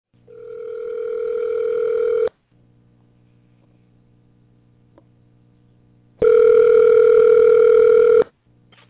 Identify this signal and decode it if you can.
Ringback tone (North America), 440 + 480 Hz.